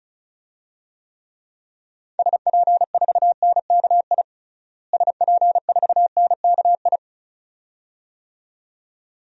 Morse code, heard as a keyed tone, about 34 wpm, 700 Hz. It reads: SP4NKI SP4NKI
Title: DXCC-100 Call Signs - 6 Characters - Repeated 2x - 35wpm